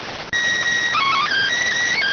SSTV (slow-scan television), PD 290.